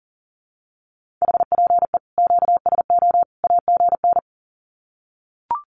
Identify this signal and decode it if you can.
Morse code, heard as a keyed tone, about 40 wpm, 700 Hz. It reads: HPE QSO AGN
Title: Common QSO Phrases 40wpm